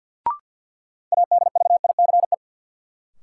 Morse code, heard as a keyed tone, about 52 wpm, 700 Hz. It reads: ADVICE
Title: Sentences from Top 1100 Words - New Word Drill - 50wpm